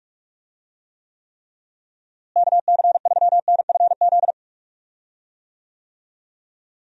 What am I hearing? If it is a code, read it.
Morse code, heard as a keyed tone, about 46 wpm, 700 Hz. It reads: KC3NFZ
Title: USA Call Signs - 6 Characters - Repeated 3x - 45wpm